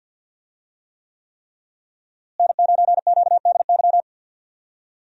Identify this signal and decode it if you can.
Morse code, heard as a keyed tone, about 51 wpm, 700 Hz. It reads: N9QDY
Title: USA Call Signs - 5 Characters - Repeated 3x - 50wpm